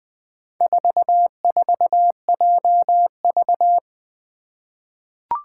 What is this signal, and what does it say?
Morse code, heard as a keyed tone, about 20 wpm, 700 Hz. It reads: 44JV
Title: Four Challenging Letter-Number Combinations 20wpm